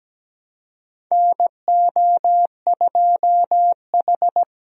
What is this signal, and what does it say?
Morse code, heard as a keyed tone, about 17 wpm, 700 Hz. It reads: NO2H
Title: USA Call Signs - 4 Characters - Repeated 2x - 17wpm